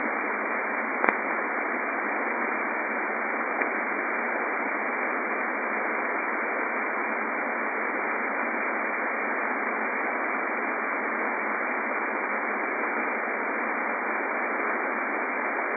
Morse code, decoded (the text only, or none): none